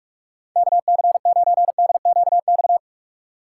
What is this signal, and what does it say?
Morse code, heard as a keyed tone, about 61 wpm, spaced as for 53 wpm, 700 Hz. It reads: KC9DQX
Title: Speed Racing - USA 2x3 Call Signs 45wpm